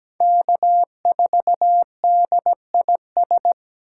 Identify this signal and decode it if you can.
Morse code, heard as a keyed tone, about 17 wpm, 700 Hz. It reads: K4DIS
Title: USA Call Signs - 5 Characters - 8x Character Spacing - 17wpm